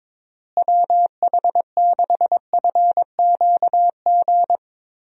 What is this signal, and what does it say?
Morse code, heard as a keyed tone, about 21 wpm, 700 Hz. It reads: WH6FQG
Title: USA Call Signs - 6 Characters - Repeated 3x - 22wpm